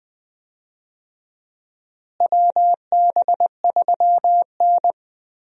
Morse code, heard as a keyed tone, about 20 wpm, 700 Hz. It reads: WB3N
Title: USA Call Signs - 4 Characters - Repeated 2x - 20wpm